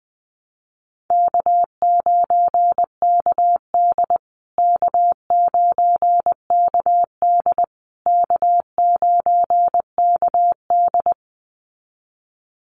Morse code, heard as a keyed tone, about 20 wpm, 700 Hz. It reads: K9KD K9KD K9KD
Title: USA Call Signs Repeated 3x 20wpm